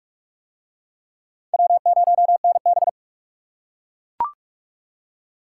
Morse code, heard as a keyed tone, about 38 wpm, 700 Hz. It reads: W0NB